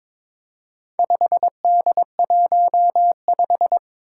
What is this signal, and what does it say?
Morse code, heard as a keyed tone, about 22 wpm, 700 Hz. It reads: 5D15
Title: Four Challenging Letter-Number Combinations 22wpm